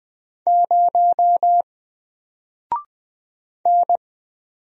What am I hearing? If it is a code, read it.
Morse code, heard as a keyed tone, about 20 wpm, 700 Hz. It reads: 0 N
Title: Single Letter-Number 20wpm